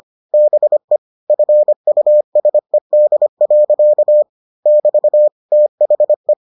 Morse code, heard as a keyed tone, about 25 wpm, 600 Hz. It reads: BE FUSED. <BT> THE